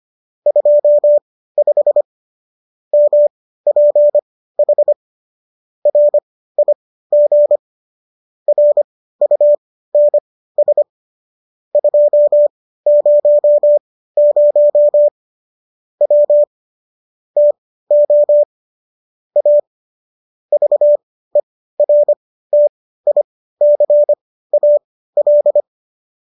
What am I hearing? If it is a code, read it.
Morse code, heard as a keyed tone, about 25 wpm, spaced as for 15 wpm, 590 Hz. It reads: 25 MPH RIG RUNS 200 W TO A VERTICAL